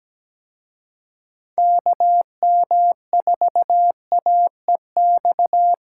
Morse code, heard as a keyed tone, about 20 wpm, 700 Hz. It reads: KM4AEX
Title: Speed Racing - USA 2x3 Call Signs 17wpm